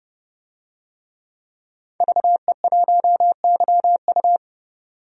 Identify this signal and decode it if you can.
Morse code, heard as a keyed tone, about 31 wpm, 700 Hz. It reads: VE1YU
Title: DXCC-100 Call Signs - 5 Characters - 30wpm